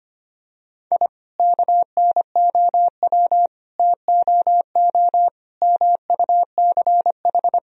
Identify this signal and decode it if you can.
Morse code, heard as a keyed tone, about 25 wpm, 700 Hz. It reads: I KNOW TOO MUCH
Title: Sentences from Top 1000 Words - Review Part 3 - 25wpm